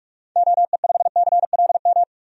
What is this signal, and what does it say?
Morse code, heard as a keyed tone, about 47 wpm, 700 Hz. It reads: OE5CLM
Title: DXCC-100 Call Signs - 6 Characters - 45wpm